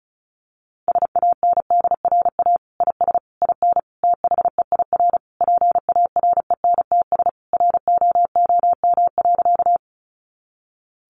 Morse code, heard as a keyed tone, about 34 wpm, 700 Hz. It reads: SANDRA IS IN THEIR PARENTS ROOM.
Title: Easy Phrases 35wpm